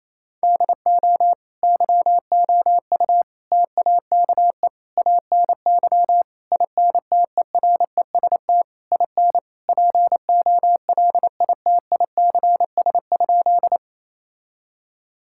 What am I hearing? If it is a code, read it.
Morse code, heard as a keyed tone, about 27 wpm, 700 Hz. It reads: DO YOU TAKE ANY INTEREST IN POLITICS?
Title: Sentences from Top 1000 Words - Review Part 6 - 28wpm